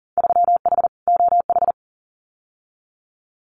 Morse code, heard as a keyed tone, about 40 wpm, 700 Hz. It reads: 3H OH
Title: Winter Field Day Exchanges Encore 40wpm